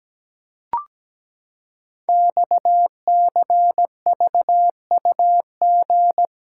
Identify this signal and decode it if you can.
Morse code, heard as a keyed tone, about 17 wpm, 700 Hz. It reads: XCVUG
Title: Code Groups - 5 Letter:Numbers - 17wpm